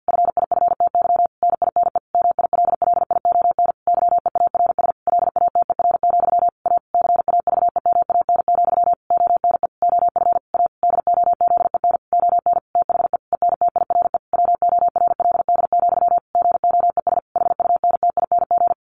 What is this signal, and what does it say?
Morse code, heard as a keyed tone, about 50 wpm, 700 Hz. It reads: FIFTY NINE MILLION YEARS LATER, A CAVEMAN, ONE OF A DOZEN ON THE ENTIRE WORLD, GOES HUNTING
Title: A Sound Of Thunder 50wpm